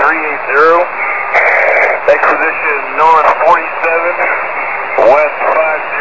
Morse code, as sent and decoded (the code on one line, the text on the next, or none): none